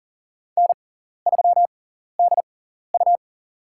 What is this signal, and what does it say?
Morse code, heard as a keyed tone, about 40 wpm, spaced as for 14 wpm, 700 Hz. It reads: N3DU
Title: USA Call Signs - 4 Characters - 6x Character Spacing - 40wpm